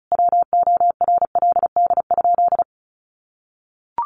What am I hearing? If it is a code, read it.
Morse code, heard as a keyed tone, about 34 wpm, 700 Hz. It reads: WORLD?
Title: Sentences from Top 1400 Words 35wpm